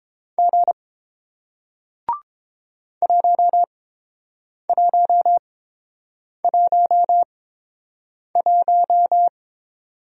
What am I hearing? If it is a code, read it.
Morse code, heard as a keyed tone, about 31 wpm, spaced as for 7 wpm, 690 Hz. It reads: G 1111